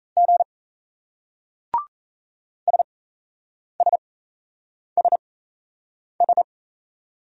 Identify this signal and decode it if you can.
Morse code, heard as a keyed tone, about 38 wpm, spaced as for 8 wpm, 700 Hz. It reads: G SSSS